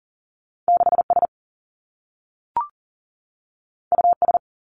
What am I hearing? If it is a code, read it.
Morse code, heard as a keyed tone, about 39 wpm, 700 Hz. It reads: 6 S U S
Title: Two Challenging Letter-Number Combinations 40wpm